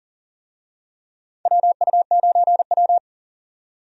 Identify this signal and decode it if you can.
Morse code, heard as a keyed tone, about 40 wpm, 700 Hz. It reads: WU9W
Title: USA Call Signs - 4 Characters - 40wpm